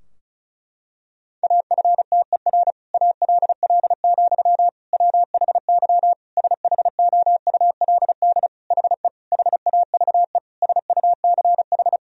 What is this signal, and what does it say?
Morse code, heard as a keyed tone, about 34 wpm, 700 Hz. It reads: AFTER ALL, WHY SHOULD HE HAVE SUCH
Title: Sentences from Top 1000 Words - Review Part 2 - 35wpm